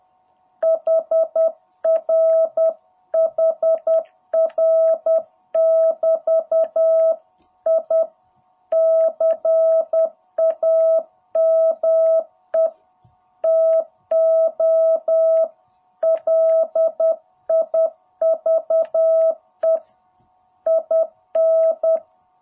Morse code, heard as a keyed tone, about 10 wpm, 660 Hz. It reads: HR HR <BT> I CAME TO LIVE IN
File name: jh8cbh / 2021cw